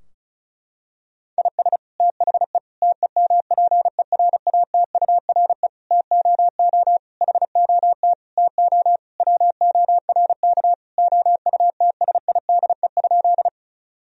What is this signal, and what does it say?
Morse code, heard as a keyed tone, about 36 wpm, 700 Hz. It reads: IS THE TEMPERATURE TOO HOT TO WORK OUTSIDE?
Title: Sentences from Top 1000 Words - Review Part 4 - 35wpm